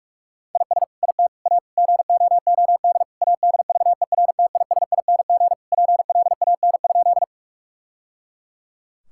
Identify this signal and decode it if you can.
Morse code, heard as a keyed tone, about 41 wpm, 700 Hz. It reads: IS IT A GOOD ADVERTISING PLAN?
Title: Sentences from Top 1000 Words - Review Part 6 - 45wpm